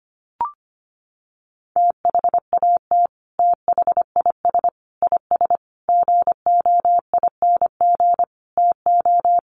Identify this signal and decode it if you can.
Morse code, heard as a keyed tone, about 25 wpm, 700 Hz. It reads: THAT THIS IS GOING TO
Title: Sets of 5 Words Encore2 25wpm